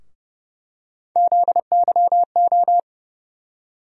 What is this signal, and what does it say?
Morse code, heard as a keyed tone, about 30 wpm, 700 Hz. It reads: ZYO
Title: USA Call Sign Postfixes - 3 Letters 30wpm